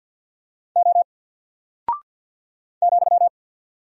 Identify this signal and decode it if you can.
Morse code, heard as a keyed tone, about 48 wpm, 700 Hz. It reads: O ,
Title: Single Character (Letter:Number:Punctuation:Prosign) 50wpm